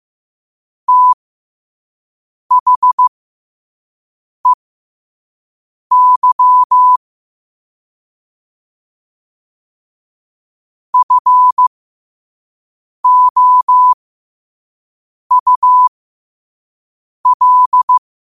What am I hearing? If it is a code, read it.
Morse code, heard as a keyed tone, about 15 wpm, spaced as for 5 wpm, 1000 Hz. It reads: THEY FOUL
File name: qotdfiles / 5wpm